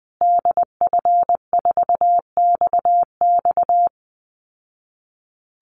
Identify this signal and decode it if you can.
Morse code, heard as a keyed tone, about 20 wpm, 700 Hz. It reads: DF4XX
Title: Call Signs Repeated 3x - 20wpm